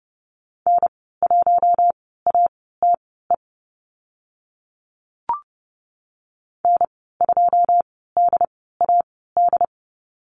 Morse code, heard as a keyed tone, about 29 wpm, spaced as for 17 wpm, 700 Hz. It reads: N1ATE N2DAD
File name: Lesson 14g - Number 2 - Call Signs 30wpm